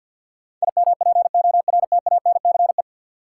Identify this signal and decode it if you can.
Morse code, heard as a keyed tone, about 47 wpm, 700 Hz. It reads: IMPORTANCE
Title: Sentences from Top 1200 Words - New Word Drill Alt - 50wpm